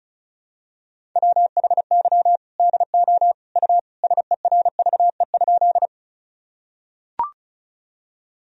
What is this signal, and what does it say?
Morse code, heard as a keyed tone, about 36 wpm, 700 Hz. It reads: WHY DO U SERVE?
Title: Sentences from Top 400 Words and Abbreviations 35wpm